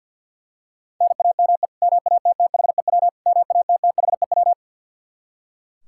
Morse code, heard as a keyed tone, about 45 wpm, 700 Hz. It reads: NAME MATTHEW MATTHEW
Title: K1USN SST QSO Practice 50wpm